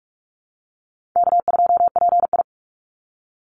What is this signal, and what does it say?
Morse code, heard as a keyed tone, about 46 wpm, 700 Hz. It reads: K2PI
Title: Call Signs - 45wpm